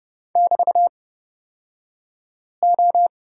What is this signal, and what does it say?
Morse code, heard as a keyed tone, about 31 wpm, spaced as for 5 wpm, 700 Hz. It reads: <BT>O
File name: Lesson 38d - Letter q - Two Characters 30wpm